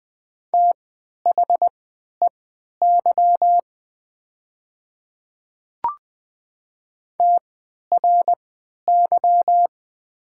Morse code, heard as a keyed tone, about 20 wpm, spaced as for 11 wpm, 700 Hz. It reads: THEY TRY
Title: 3x Character Spacing - Top 200 Words - 20wpm